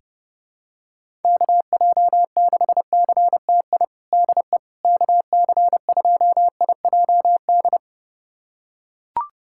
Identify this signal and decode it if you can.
Morse code, heard as a keyed tone, about 30 wpm, 700 Hz. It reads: KJ6CTI DE KC2IJD
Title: USA Callsign DE Callsign - 6 Characters - 30wpm